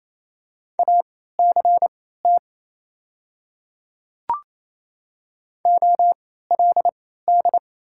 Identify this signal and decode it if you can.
Morse code, heard as a keyed tone, about 28 wpm, spaced as for 16 wpm, 700 Hz. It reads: ACT OLD
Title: 3x Character Spacing - Top 200 Words - 28wpm